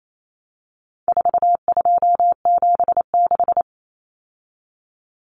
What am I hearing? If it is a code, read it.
Morse code, heard as a keyed tone, about 28 wpm, 700 Hz. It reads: 4276
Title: Common Cut Numbers - 4-Digit 28wpm